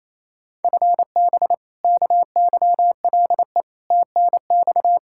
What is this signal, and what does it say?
Morse code, heard as a keyed tone, about 28 wpm, 700 Hz. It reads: FB KYLE TNX
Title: Practice QSO - Exchange 3 - More Info - 28wpm